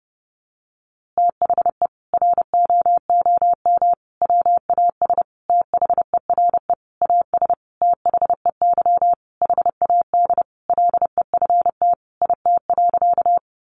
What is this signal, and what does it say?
Morse code, heard as a keyed tone, about 30 wpm, 700 Hz. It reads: THE ROOM WAS THERE AS THEY HAD LEFT IT.
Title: A Sound Of Thunder 30wpm